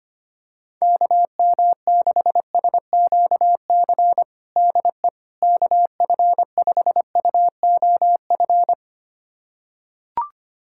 Morse code, heard as a keyed tone, about 25 wpm, 700 Hz. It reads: KM6SQC DE KF5UOF
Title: USA Callsign DE Callsign - 6 Characters - 25wpm